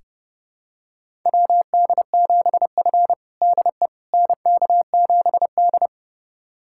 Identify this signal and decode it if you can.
Morse code, heard as a keyed tone, about 30 wpm, 700 Hz. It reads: WD7F DE NK7D
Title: USA Callsign DE Callsign - 4 Characters - 30wpm